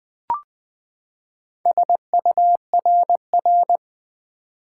Morse code, heard as a keyed tone, about 20 wpm, 700 Hz. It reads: SURR
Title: Build A Word - 4 Letters At A Time - 11-Letter Words - 20wpm